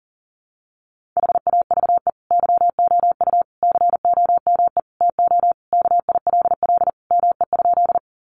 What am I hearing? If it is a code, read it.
Morse code, heard as a keyed tone, about 37 wpm, 700 Hz. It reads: HAVE YOU COME TO KILL ME?
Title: Sentences from Top 1400 Words 40wpm